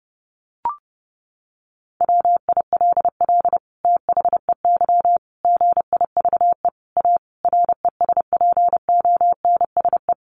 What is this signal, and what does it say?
Morse code, heard as a keyed tone, about 30 wpm, 700 Hz. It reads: WILL THEY GIVE A RESPONSE
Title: Sentences from Top 1400 Words 30wpm